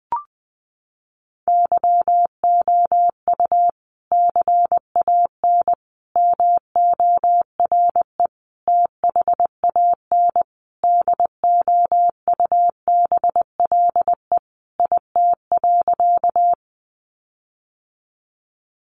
Morse code, heard as a keyed tone, about 20 wpm, 700 Hz. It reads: YOU CAN MORE THAN DOUBLE IT.
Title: Sentences from Top 1400 Words 20wpm